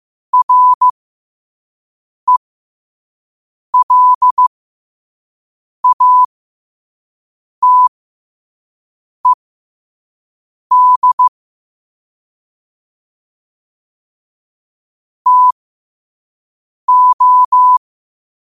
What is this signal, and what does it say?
Morse code, heard as a keyed tone, about 15 wpm, spaced as for 5 wpm, 1000 Hz. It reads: RELATED TO
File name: qotdfiles / 5wpm